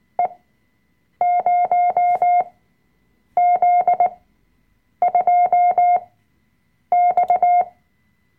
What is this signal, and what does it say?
Morse code, heard as a keyed tone, about 19 wpm, spaced as for 7 wpm, 710 Hz. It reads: E0Z2X
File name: Workshop Dateien / CW-Online-Kurs / Lektion 005